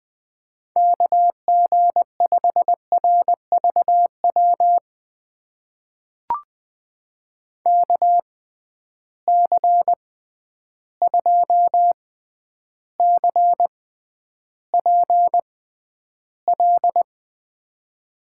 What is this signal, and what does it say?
Morse code, heard as a keyed tone, about 20 wpm, spaced as for 10 wpm, 700 Hz. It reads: KG5RVW KC2CPL
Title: USA Call Signs - 6 Characters - 6x Character Spacing - 20wpm